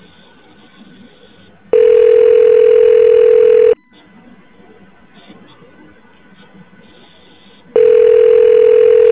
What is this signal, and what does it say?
Ringback tone (North America), 440 + 480 Hz.